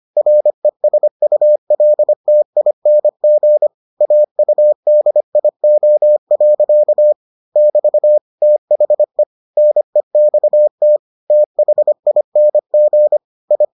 Morse code, heard as a keyed tone, about 25 wpm, 600 Hz. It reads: RESULTING AUDIO. <BT> THE NEXT THING I